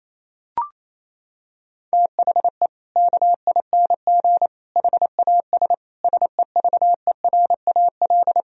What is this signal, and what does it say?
Morse code, heard as a keyed tone, about 28 wpm, 700 Hz. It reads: THE KING HAS SEVERAL
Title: Sentences from Top 1000 Words - Review Part 2 - 28wpm